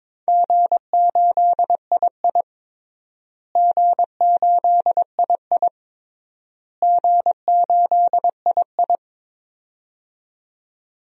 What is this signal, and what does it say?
Morse code, heard as a keyed tone, about 22 wpm, 700 Hz. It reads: G8II G8II G8II
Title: DXCC-100 Call Signs - 4 Characters - Repeated 3x - 22wpm